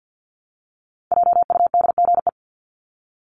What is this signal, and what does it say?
Morse code, heard as a keyed tone, about 50 wpm, 700 Hz. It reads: JUDGE
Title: Top 1751-2000 Words 50wpm